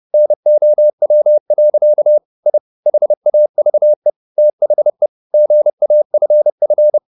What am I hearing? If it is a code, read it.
Morse code, heard as a keyed tone, about 30 wpm, 600 Hz. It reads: NOW. I HAVE THE GAFF